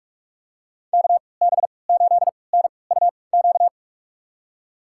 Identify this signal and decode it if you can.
Morse code, heard as a keyed tone, about 43 wpm, spaced as for 25 wpm, 690 Hz. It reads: KB8NUQ
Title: USA Call Signs - 6 Characters - 3x Character Spacing - 45wpm